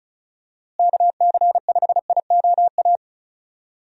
Morse code, heard as a keyed tone, about 36 wpm, 700 Hz. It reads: KC5IOA
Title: USA Call Signs - 6 Characters - Repeated 3x - 35wpm